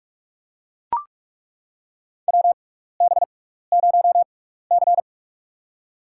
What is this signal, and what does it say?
Morse code, heard as a keyed tone, about 42 wpm, spaced as for 15 wpm, 700 Hz. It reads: WB0C
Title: USA Call Signs - 6x Character Spacing - 45wpm